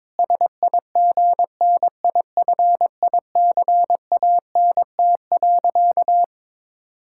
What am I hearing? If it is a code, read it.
Morse code, heard as a keyed tone, about 22 wpm, 700 Hz. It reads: SIGNIFICANT.
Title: Sentences from Top 1000 Words - Review Part 6 - 22wpm